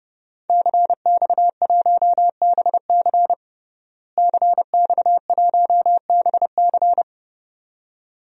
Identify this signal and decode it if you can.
Morse code, heard as a keyed tone, about 30 wpm, 700 Hz. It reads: CX1BC CX1BC
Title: DXCC-100 Call Signs - 5 Characters - Repeated 2x - 30wpm